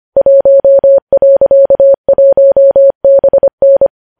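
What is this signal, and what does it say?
Morse code, heard as a keyed tone, about 25 wpm, 550 Hz. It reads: 1.1BN
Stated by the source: News headlines sent in Morse code at 25.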